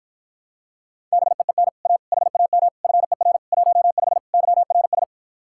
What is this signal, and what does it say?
Morse code, heard as a keyed tone, about 54 wpm, 700 Hz. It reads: BEEN A HAM FER 15 YRS